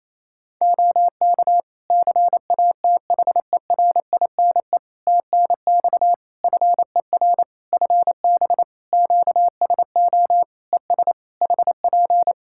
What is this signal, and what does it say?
Morse code, heard as a keyed tone, about 28 wpm, 700 Hz. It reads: OK CATHERINE TNX FER FB QSO ES HP
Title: Practice QSO - Exchange 4 - Ending - 28wpm